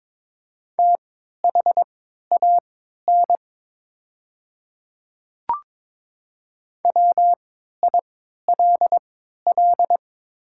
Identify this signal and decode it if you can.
Morse code, heard as a keyed tone, about 22 wpm, spaced as for 12 wpm, 700 Hz. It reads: THAN WILL
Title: 3x Character Spacing - Top 100 Words - 22wpm